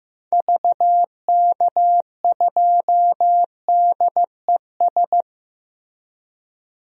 Morse code, heard as a keyed tone, about 16 wpm, 700 Hz. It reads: VK2DES
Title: DXCC-100 Call Signs - 6 Characters - Repeated 2x - 15wpm